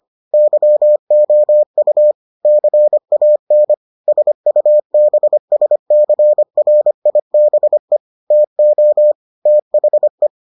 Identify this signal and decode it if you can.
Morse code, heard as a keyed tone, about 25 wpm, 600 Hz. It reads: YOU CAN SUBSCRIBE TO THE